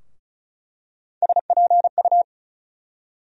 Morse code, heard as a keyed tone, about 36 wpm, 700 Hz. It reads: SPU
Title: USA Call Sign Postfixes - 3 Letters 35wpm